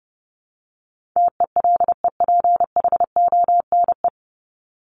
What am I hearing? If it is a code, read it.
Morse code, heard as a keyed tone, about 30 wpm, 700 Hz. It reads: TELEPHONE
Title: Top 1501-1750 Words 30wpm